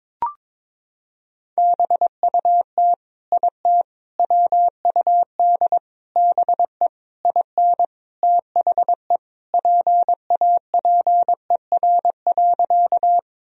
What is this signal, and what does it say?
Morse code, heard as a keyed tone, about 22 wpm, 700 Hz. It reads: BUT IT WUD BE IN THE PAPER.
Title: Sentences from Top 300 Words and Abbreviations 22wpm